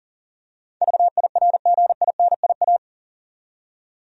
Morse code, heard as a keyed tone, about 40 wpm, 700 Hz. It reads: VIRGINIA
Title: USA Common Names - 40wpm